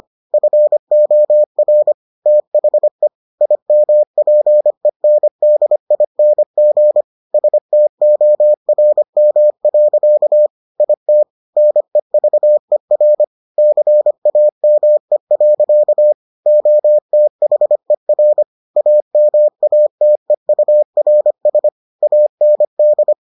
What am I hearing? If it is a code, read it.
Morse code, heard as a keyed tone, about 25 wpm, 600 Hz. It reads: FOR THE IMPENDING STORM. IT NEVER CAME. OTHER AMATEURS AND